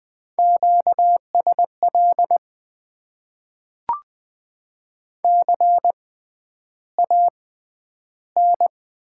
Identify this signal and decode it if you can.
Morse code, heard as a keyed tone, about 20 wpm, spaced as for 10 wpm, 700 Hz. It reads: QSL CAN
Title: Top 90 QSO Elements - 6x Character Spacing - 20wpm